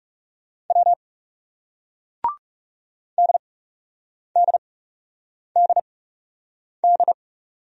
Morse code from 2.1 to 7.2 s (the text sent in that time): DDDD